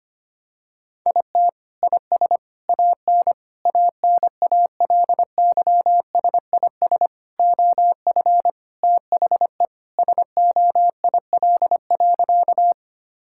Morse code, heard as a keyed tone, about 25 wpm, 700 Hz. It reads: IT IS AN ANALYSIS OF THE SOIL.
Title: Sentences from Top 1000 Words - Review Part 5 - 25wpm